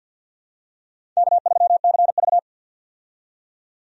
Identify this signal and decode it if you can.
Morse code, heard as a keyed tone, about 49 wpm, 700 Hz. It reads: K3CV
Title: USA Call Signs - 4 Characters - 50wpm